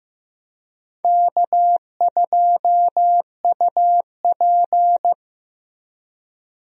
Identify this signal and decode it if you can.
Morse code, heard as a keyed tone, about 15 wpm, 700 Hz. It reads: K2UP
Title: USA Call Signs - 4 Characters - Repeated 3x - 15wpm